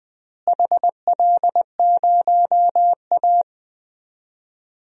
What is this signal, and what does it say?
Morse code, heard as a keyed tone, about 20 wpm, 700 Hz. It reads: HL0A
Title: DXCC-100 Call Signs - 4 Characters - Repeated 2x - 20wpm